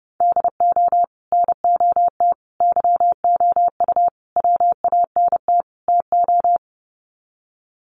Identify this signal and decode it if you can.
Morse code, heard as a keyed tone, about 30 wpm, 700 Hz. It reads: DO NOT YOU WANT TO
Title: Sets of 5 Words Encore2 30wpm